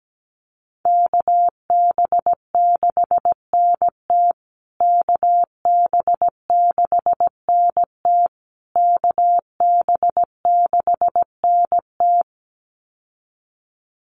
Morse code, heard as a keyed tone, about 17 wpm, 700 Hz. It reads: KB6NT KB6NT KB6NT
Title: USA Call Signs Repeated 3x 17wpm